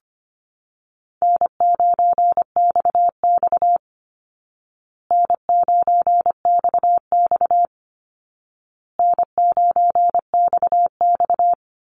Morse code, heard as a keyed tone, about 25 wpm, 700 Hz. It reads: N9XX N9XX N9XX
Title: Call Signs Repeated 2x - 25wpm